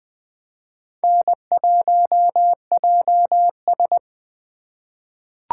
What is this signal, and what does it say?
Morse code, heard as a keyed tone, about 20 wpm, 700 Hz. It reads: N1JS
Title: USA Call Signs - 3x Character Spacing - 20wpm